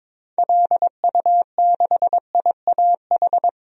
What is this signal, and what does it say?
Morse code, heard as a keyed tone, about 22 wpm, 700 Hz. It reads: LU6IAH
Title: DXCC-100 Call Signs - 6 Characters - Repeated 2x - 22wpm